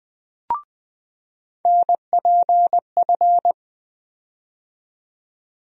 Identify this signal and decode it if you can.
Morse code, heard as a keyed tone, about 20 wpm, 700 Hz. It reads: NPF
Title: USA Call Sign Postfixes - 3 Letters 20wpm